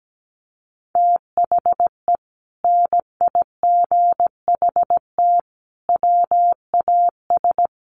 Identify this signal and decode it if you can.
Morse code, heard as a keyed tone, about 17 wpm, 700 Hz. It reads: THE NIGHT WAS
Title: Sentences from Top 400 Words 17wpm